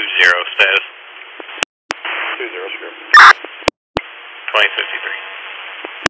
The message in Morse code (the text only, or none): none